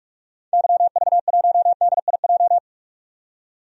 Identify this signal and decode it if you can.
Morse code, heard as a keyed tone, about 47 wpm, 700 Hz. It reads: YV1DIJ
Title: DXCC-100 Call Signs - 6 Characters - 45wpm